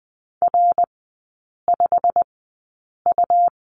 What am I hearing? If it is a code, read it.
Morse code, heard as a keyed tone, about 20 wpm, spaced as for 8 wpm, 700 Hz. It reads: R5U
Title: 107 Lesson 18e - Letter m - Three Characters 20wpm